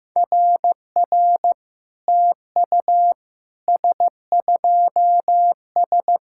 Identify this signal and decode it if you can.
Morse code, heard as a keyed tone, about 15 wpm, 700 Hz. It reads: RR TU S2S
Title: SOTA QSO Practice 15wpm